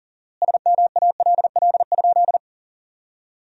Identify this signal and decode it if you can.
Morse code, heard as a keyed tone, about 40 wpm, 700 Hz. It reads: SMALL?
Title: 4x Word-Spacing - Sentences from Top 200 Words - 40wpm